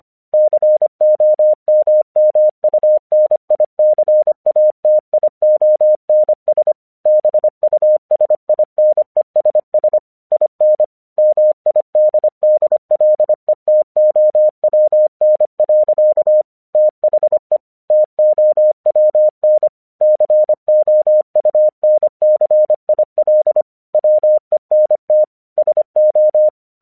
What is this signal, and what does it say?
Morse code, heard as a keyed tone, about 25 wpm, 610 Hz. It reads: COMMUNICATIONS BUSINESS IN MIDDLETOWN. THE TOWN COUNCIL WENT SO
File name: 1 English 25wpm / AQC07